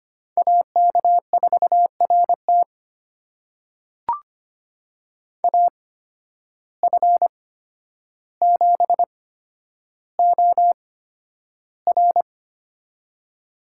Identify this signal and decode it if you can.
Morse code, heard as a keyed tone, about 25 wpm, spaced as for 11 wpm, 700 Hz. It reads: AK4RT AF7OR
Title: USA Call Signs - 8x Character Spacing - 25wpm